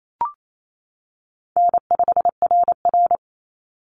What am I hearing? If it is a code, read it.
Morse code, heard as a keyed tone, about 28 wpm, 700 Hz. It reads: N5RR
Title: Easy Call Signs - 28wpm